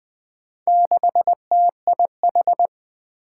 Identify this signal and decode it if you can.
Morse code, heard as a keyed tone, about 20 wpm, 700 Hz. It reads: 6TIH
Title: Four Challenging Letter-Number Combinations 20wpm